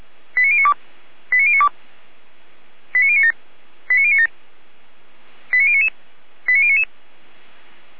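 Five-tone selective call (ZVEI), heard as quick runs of five tones, tones of about 70 ms. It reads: 89092 89092 89097 89097 89099 89099
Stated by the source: ZVEI five-tone selective call received by SDR.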